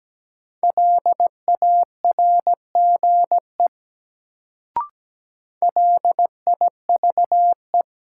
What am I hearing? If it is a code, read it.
Morse code, heard as a keyed tone, about 17 wpm, 700 Hz. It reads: LARGE LIVE
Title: Sentences from Top 200 Words - New Word Drill - 17wpm